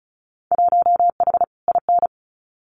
Morse code, heard as a keyed tone, about 36 wpm, 700 Hz. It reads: 1H IN
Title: Winter Field Day Exchanges 35wpm